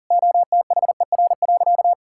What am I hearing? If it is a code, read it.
Morse code, heard as a keyed tone, about 40 wpm, 700 Hz. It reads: OTHER.
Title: Sentences from Top 1000 Words - Review Part 5 - 40wpm